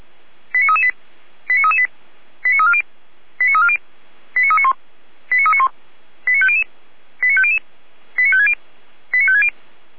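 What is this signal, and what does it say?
Five-tone selective call (ZVEI), heard as quick runs of five tones, tones of about 70 ms. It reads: 89338 89338 89340 89340 89391 89391 89500 89500 89560 89560
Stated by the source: ZVEI five-tone call 89325 received with an SDR.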